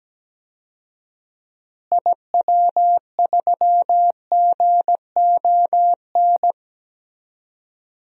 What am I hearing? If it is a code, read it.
Morse code, heard as a keyed tone, about 17 wpm, 700 Hz. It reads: IW3GON
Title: DXCC-100 Call Signs - 6 Characters - Repeated 3x - 17wpm